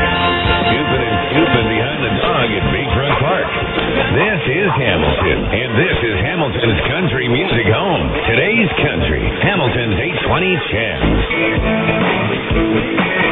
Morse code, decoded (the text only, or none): none